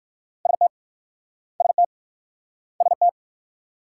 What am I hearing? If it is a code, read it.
Morse code, heard as a keyed tone, about 56 wpm, 690 Hz. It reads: ST ST ST
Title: Speed-Racing Multi-letter Phonemes - 45wpm